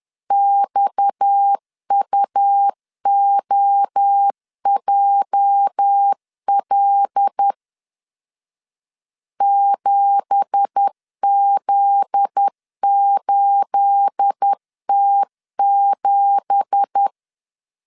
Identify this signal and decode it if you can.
Morse code, heard as a keyed tone, about 10 wpm, 800 Hz. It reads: XUOJL 7Z8T7
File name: CW / DET / 1 9 1500C S A 10wpm 4511